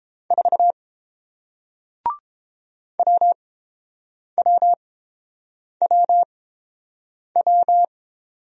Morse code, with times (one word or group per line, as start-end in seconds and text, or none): 0.3-0.7 4
3.0-7.8 WWWW